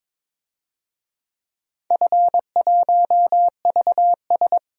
Morse code, heard as a keyed tone, about 22 wpm, 700 Hz. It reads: F1VS
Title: DXCC-100 Call Signs - 4 Characters - Repeated 2x - 22wpm